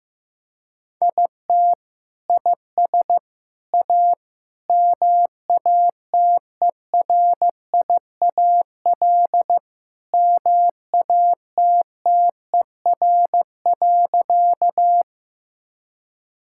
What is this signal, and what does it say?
Morse code, heard as a keyed tone, about 15 wpm, 700 Hz. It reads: IT IS A MATERIAL MATTER.
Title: Sentences from Top 1000 Words - Review Part 5 - 15wpm